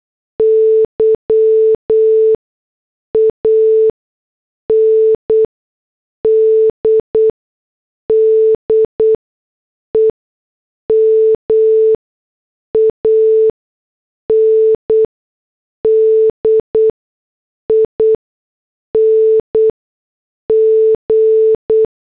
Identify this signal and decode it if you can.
Morse code, heard as a keyed tone, about 8 wpm, spaced as for 6 wpm, 440 Hz. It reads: Y AND DEMANDING